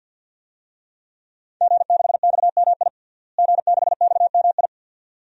Morse code, heard as a keyed tone, about 48 wpm, 700 Hz. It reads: G6XMI G6XMI
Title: DXCC-100 Call Signs - 5 Characters - Repeated 2x - 50wpm